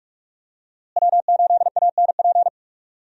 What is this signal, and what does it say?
Morse code, heard as a keyed tone, about 46 wpm, 700 Hz. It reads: W8ANP
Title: USA Call Signs - 5 Characters - Repeated 2x - 45wpm